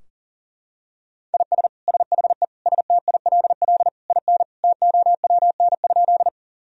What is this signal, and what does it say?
Morse code, heard as a keyed tone, about 40 wpm, 700 Hz. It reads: IS SHE STILL IN TOWN?
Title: Sentences from Top 1000 Words - Review Part 2 - 40wpm